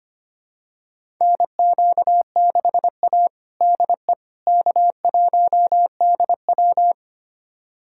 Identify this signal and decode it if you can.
Morse code, heard as a keyed tone, about 25 wpm, 700 Hz. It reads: NQ6A DE K1DW
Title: USA Callsign DE Callsign - 4 Characters - 25wpm